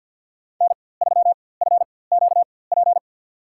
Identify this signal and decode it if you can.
Morse code, heard as a keyed tone, about 48 wpm, spaced as for 23 wpm, 700 Hz. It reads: N3FQP
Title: USA Call Signs - 5 Characters - 4x Character Spacing - 50wpm